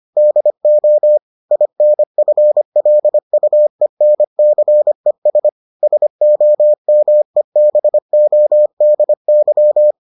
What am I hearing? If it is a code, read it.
Morse code, heard as a keyed tone, about 25 wpm, 600 Hz. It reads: DO INFLUENCES SOMEBODY